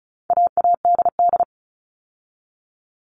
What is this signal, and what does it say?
Morse code, heard as a keyed tone, about 34 wpm, 700 Hz. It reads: AADD
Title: All Cut Numbers - 4-Digit 35wpm